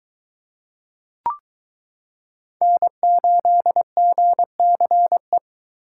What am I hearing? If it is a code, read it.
Morse code, heard as a keyed tone, about 22 wpm, 700 Hz. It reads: N8GCE